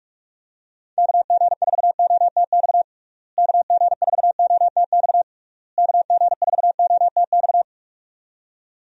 Morse code, heard as a keyed tone, about 46 wpm, 700 Hz. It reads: KG4OTX KG4OTX KG4OTX
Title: USA Call Signs - 6 Characters - Repeated 3x - 45wpm